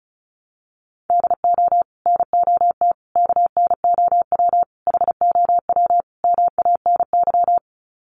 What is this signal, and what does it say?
Morse code, heard as a keyed tone, about 34 wpm, 700 Hz. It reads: DO NOT KNOW HOW MANY
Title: Sets of 5 Words Encore 35wpm